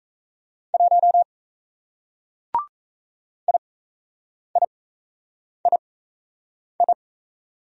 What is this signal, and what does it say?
Morse code, heard as a keyed tone, about 37 wpm, spaced as for 8 wpm, 700 Hz. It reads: 1 IIII